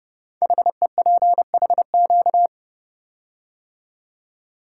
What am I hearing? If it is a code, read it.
Morse code, heard as a keyed tone, about 30 wpm, 700 Hz. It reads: HEPHQ
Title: Code Groups - 5 Letters - 30wpm